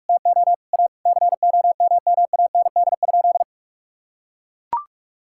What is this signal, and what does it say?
Morse code, heard as a keyed tone, about 41 wpm, 700 Hz. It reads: TO A COMMAND?
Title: Sentences from Top 1600 Words 45wpm